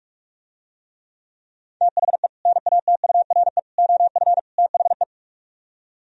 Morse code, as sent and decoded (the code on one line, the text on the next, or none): - .... / -. .- - ..- .-. / --- ..-. / - .... .
THE NATURE OF THE